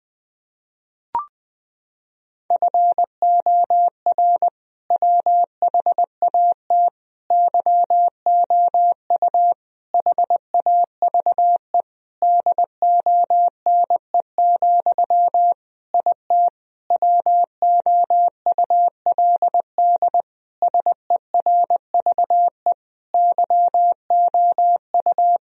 Morse code, heard as a keyed tone, about 20 wpm, 700 Hz. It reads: FOR WHAT YOU HAVE DONE, IT WOULD SERVE YOU
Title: Sentences from Top 1000 Words - Review Part 2 - 20wpm